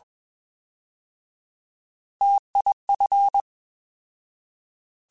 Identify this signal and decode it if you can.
Morse code, heard as a keyed tone, about 21 wpm, 800 Hz. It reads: TIF